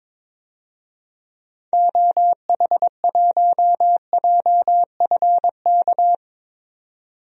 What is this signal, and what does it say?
Morse code, heard as a keyed tone, about 21 wpm, 700 Hz. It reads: OH1JFK
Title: DXCC-100 Call Signs - 6 Characters - 22wpm